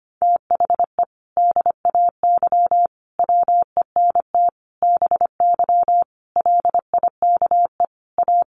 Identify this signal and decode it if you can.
Morse code, heard as a keyed tone, about 25 wpm, 700 Hz. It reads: THE DAY WENT BY LIKE A
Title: Sentences from Top 1500 Words 25wpm